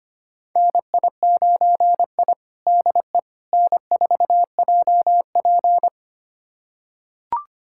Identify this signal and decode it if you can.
Morse code, heard as a keyed tone, about 25 wpm, 700 Hz. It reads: NI9I DE N4JP
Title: USA Callsign DE Callsign - 4 Characters - 25wpm